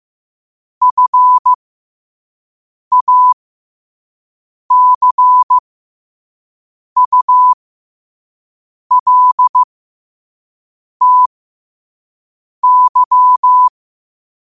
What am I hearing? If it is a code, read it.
Morse code, heard as a keyed tone, about 15 wpm, spaced as for 5 wpm, 1000 Hz. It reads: FACULTY
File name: qotdfiles / 5wpm